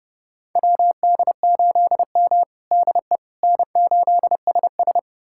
Morse code, heard as a keyed tone, about 30 wpm, 700 Hz. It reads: WD8M DE N8SS
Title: USA Callsign DE Callsign - 4 Characters - 30wpm